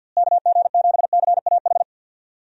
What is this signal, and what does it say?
Morse code, heard as a keyed tone, about 52 wpm, 700 Hz. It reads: KG7CAH
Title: USA Call Signs - 6 Characters - Repeated 2x - 50wpm